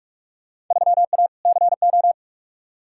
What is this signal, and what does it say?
Morse code, heard as a keyed tone, about 46 wpm, 700 Hz. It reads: 3A CO
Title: Field Day Exchanges Encore - 45wpm